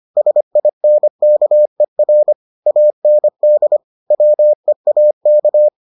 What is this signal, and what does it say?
Morse code, heard as a keyed tone, about 25 wpm, 600 Hz. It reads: SINKER AND WEAK